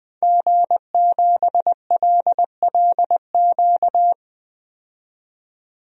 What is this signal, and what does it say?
Morse code, heard as a keyed tone, about 20 wpm, 700 Hz. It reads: G7LLQ
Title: DXCC-100 Call Signs - 5 Characters - 20wpm